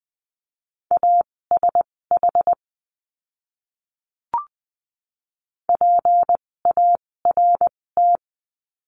Morse code, heard as a keyed tone, about 20 wpm, spaced as for 16 wpm, 700 Hz. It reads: ASH PART
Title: 256 Lesson 42a - Words 12-20wpm